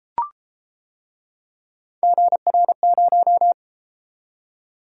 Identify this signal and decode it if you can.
Morse code, heard as a keyed tone, about 31 wpm, 700 Hz. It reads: GR0